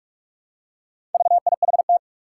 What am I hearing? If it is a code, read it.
Morse code, heard as a keyed tone, about 44 wpm, 700 Hz. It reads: VIHT
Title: Four Challenging Letter-Number Combinations 45wpm